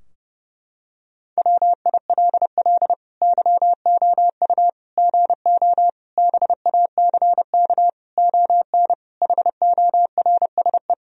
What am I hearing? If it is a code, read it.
Morse code, heard as a keyed tone, about 30 wpm, 700 Hz. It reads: WILL YOU GO BACK ON HORSE
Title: Sentences from Top 1000 Words - Review Part 3 - 30wpm